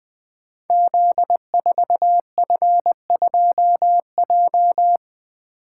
Morse code, heard as a keyed tone, about 20 wpm, 700 Hz. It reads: Z4F2J
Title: Code Groups - 5 Letter:Numbers - 20wpm